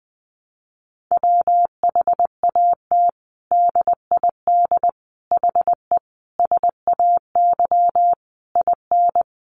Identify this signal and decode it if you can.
Morse code, heard as a keyed tone, about 20 wpm, 700 Hz. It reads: WHAT DID HE SAY IN
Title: Sentences from Top 300 Words 20wpm